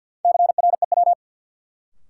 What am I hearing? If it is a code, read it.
Morse code, heard as a keyed tone, about 53 wpm, 700 Hz. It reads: CREW
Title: Sentences from Top 1500 Words - New Word Drill - 50wpm